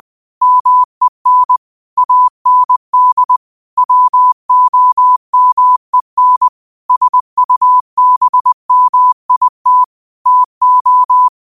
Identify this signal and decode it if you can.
Morse code, heard as a keyed tone, about 20 wpm, 1000 Hz. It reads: MEN AND WOMEN SUBMIT TO